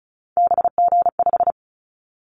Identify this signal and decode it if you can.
Morse code, heard as a keyed tone, about 34 wpm, 700 Hz. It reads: BG5
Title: Callsign Prefixes for Top 100 DXCC - 35wpm